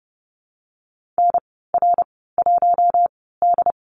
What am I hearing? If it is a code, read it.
Morse code, heard as a keyed tone, about 30 wpm, spaced as for 17 wpm, 700 Hz. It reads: NR1D
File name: Lesson 14g - Number 2 - Call Signs 30wpm